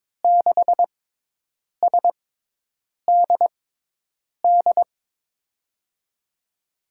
Morse code, heard as a keyed tone, about 22 wpm, spaced as for 8 wpm, 700 Hz. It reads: 6SDD
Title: USA Call Signs - 6 Characters - 6x Character Spacing - 22wpm